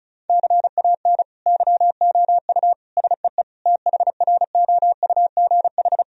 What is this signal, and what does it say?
Morse code, heard as a keyed tone, about 34 wpm, 700 Hz. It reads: CAN YOU SEE THROUGH
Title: Sentences from Top 1000 Words - Review Part 1 - 35wpm